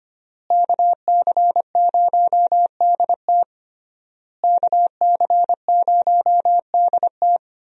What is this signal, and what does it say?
Morse code, heard as a keyed tone, about 25 wpm, 700 Hz. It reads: KC0DT KC0DT
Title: USA Call Signs - 5 Characters - Repeated 3x - 25wpm